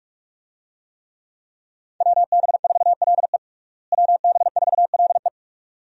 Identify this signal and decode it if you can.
Morse code, heard as a keyed tone, about 46 wpm, 700 Hz. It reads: WB4LE WB4LE
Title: USA Call Signs - 5 Characters - Repeated 2x - 45wpm